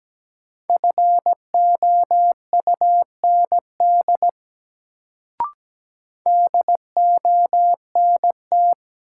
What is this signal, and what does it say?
Morse code, heard as a keyed tone, about 17 wpm, 700 Hz. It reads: FOUND DONT
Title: Sentences from Top 300 Words - New Word Drill - 17wpm